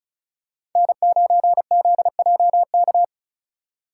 Morse code, heard as a keyed tone, about 36 wpm, 700 Hz. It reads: N9ZJK
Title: USA Call Signs - 4x Character Spacing - 35wpm